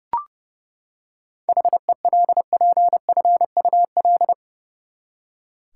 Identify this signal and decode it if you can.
Morse code, heard as a keyed tone, about 30 wpm, 700 Hz. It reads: HELPFUL
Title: Seven Letter Words - 30wpm